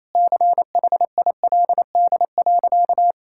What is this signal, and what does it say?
Morse code, heard as a keyed tone, about 28 wpm, 700 Hz. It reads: CHILD.
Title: Sentences from Top 1000 Words - Review Part 6 - 28wpm